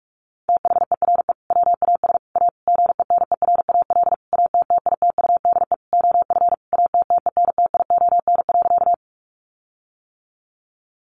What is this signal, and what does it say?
Morse code, heard as a keyed tone, about 46 wpm, 700 Hz. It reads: THERE WAS A GENERAL ATTITUDE OF ATTENTION.
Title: Sentences from Top 1300 Words 45wpm